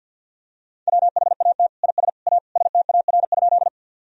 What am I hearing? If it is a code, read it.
Morse code, heard as a keyed tone, about 52 wpm, 700 Hz. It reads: WHAT IS A STAR?
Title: Sentences from Top 1000 Words - Review Part 2 - 50wpm